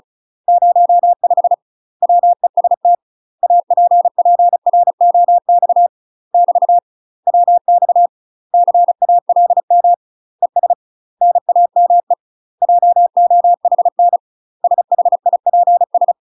Morse code, heard as a keyed tone, about 36 wpm, 700 Hz. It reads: 05 WEST APPROX <BT> WX CALM ES NAME JOHN SHIPS
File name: UK MASTER QSOs TRAINING/35WPM